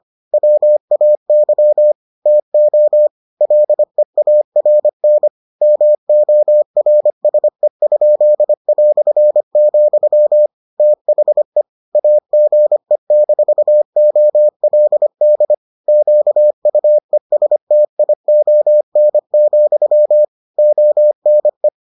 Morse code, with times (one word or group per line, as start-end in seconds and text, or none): 0.3-1.9 WAY
2.3-3.1 TO
3.4-5.3 LEARN
5.6-10.5 MORSE?",
10.8-11.6 THE
11.9-15.5 AGE-OLD
15.9-20.2 QUESTION,
20.6-21.7 ONE